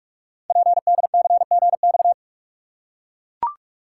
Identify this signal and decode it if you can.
Morse code, heard as a keyed tone, about 44 wpm, 700 Hz. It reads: PDCGX
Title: Code Groups - 5 Letters - 45wpm